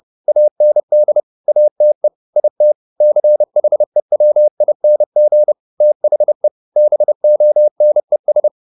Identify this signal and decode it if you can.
Morse code, heard as a keyed tone, about 30 wpm, 600 Hz. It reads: AND ATE IT CHEWING THE BONES